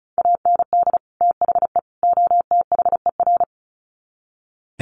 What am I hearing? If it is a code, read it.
Morse code, heard as a keyed tone, about 34 wpm, 700 Hz. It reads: AND THE OTHER
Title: Sets of 3 Words 35wpm